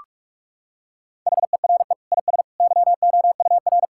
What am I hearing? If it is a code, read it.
Morse code, heard as a keyed tone, about 46 wpm, 700 Hz. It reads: HERE IS YOUR
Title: Sentences from Top 1600 Words 45wpm